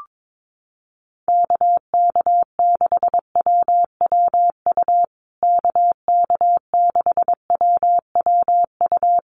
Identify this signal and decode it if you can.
Morse code, heard as a keyed tone, about 22 wpm, 700 Hz. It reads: KK6WWU KK6WWU
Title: USA Call Signs Repeated 3x 22wpm